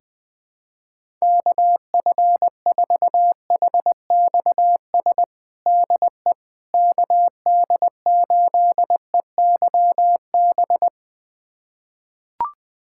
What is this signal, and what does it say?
Morse code, heard as a keyed tone, about 20 wpm, 700 Hz. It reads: KF4HXS DE KD8EYB
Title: USA Callsign DE Callsign - 6 Characters - 20wpm